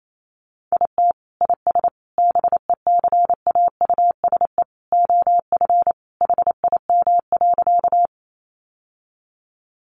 Morse code, heard as a keyed tone, about 28 wpm, 700 Hz. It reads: IT IS BECAUSE OF HIM.
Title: Sentences from Top 600 Words 28wpm